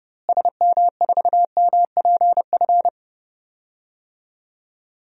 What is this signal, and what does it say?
Morse code, heard as a keyed tone, about 32 wpm, 700 Hz. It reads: SM4MPF
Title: DXCC-100 Call Signs - 6 Characters - Repeated 2x - 30wpm